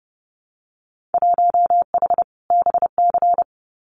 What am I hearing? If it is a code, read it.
Morse code, heard as a keyed tone, about 30 wpm, 700 Hz. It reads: 1H BC
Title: Winter Field Day Exchanges 30wpm